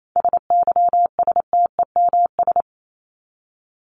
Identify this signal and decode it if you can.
Morse code, heard as a keyed tone, about 28 wpm, 700 Hz. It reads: SYSTEMS
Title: ES:S Suffix - Top 100 Words with Base Words 28wpm